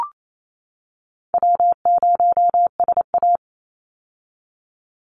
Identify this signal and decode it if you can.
Morse code, heard as a keyed tone, about 28 wpm, 700 Hz. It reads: W0SA
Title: Easy Call Signs - 28wpm